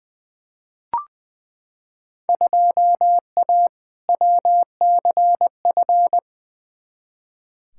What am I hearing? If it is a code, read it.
Morse code, heard as a keyed tone, about 20 wpm, 700 Hz. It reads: 2A WCF
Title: Field Day Exchanges Encore - 20wpm